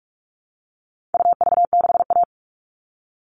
Morse code, heard as a keyed tone, about 44 wpm, 700 Hz. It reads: UV6A
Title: All Cut Numbers - 4-Digit 45wpm